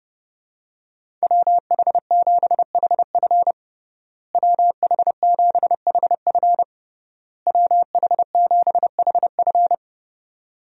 Morse code, heard as a keyed tone, about 30 wpm, 700 Hz. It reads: WH7HF WH7HF WH7HF
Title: USA Call Signs - 5 Characters - Repeated 3x - 30wpm